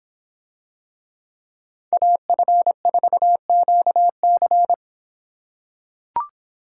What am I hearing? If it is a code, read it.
Morse code, heard as a keyed tone, about 26 wpm, 700 Hz. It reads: AF4QC